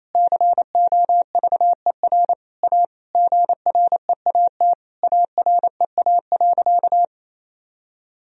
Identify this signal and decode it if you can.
Morse code, heard as a keyed tone, about 27 wpm, 700 Hz. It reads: COVER A GREAT AREA.
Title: Sentences from Top 1000 Words - Review Part 2 - 28wpm